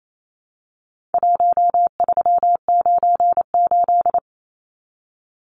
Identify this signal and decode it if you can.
Morse code, heard as a keyed tone, about 28 wpm, 700 Hz. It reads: 1398
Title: Numbers - Contest Style - 4-Digit 28wpm